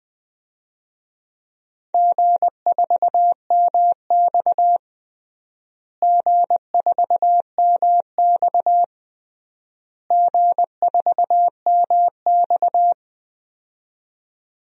Morse code, heard as a keyed tone, about 20 wpm, 700 Hz. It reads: G4MX G4MX G4MX
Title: DXCC-100 Call Signs - 4 Characters - Repeated 3x - 20wpm